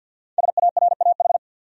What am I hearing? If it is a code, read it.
Morse code, heard as a keyed tone, about 49 wpm, 700 Hz. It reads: SARAH
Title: USA Common Names - Max 5 Letters - 50wpm